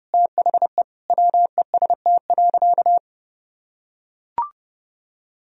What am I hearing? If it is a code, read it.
Morse code, heard as a keyed tone, about 30 wpm, 700 Hz. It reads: THE WEST.
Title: Sentences from Top 1000 Words - Review Part 2 - 30wpm